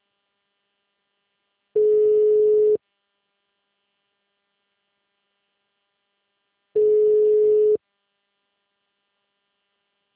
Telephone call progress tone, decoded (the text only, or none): ringback tone